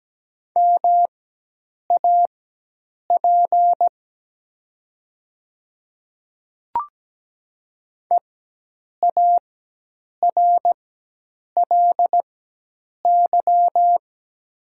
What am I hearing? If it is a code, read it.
Morse code, heard as a keyed tone, about 17 wpm, spaced as for 8 wpm, 700 Hz. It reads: MAP EARLY
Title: 4x Character Spacing - Top 500 Words - 17wpm